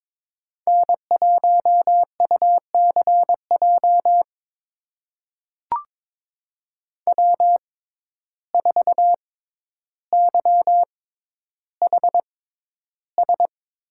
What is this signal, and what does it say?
Morse code, heard as a keyed tone, about 22 wpm, spaced as for 11 wpm, 700 Hz. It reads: N1UCJ W4YHS
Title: USA Call Signs - 5 Characters - 6x Character Spacing - 22wpm